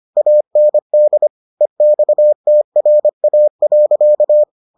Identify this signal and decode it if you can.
Morse code, heard as a keyed tone, about 25 wpm, 600 Hz. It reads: AND EXTRA.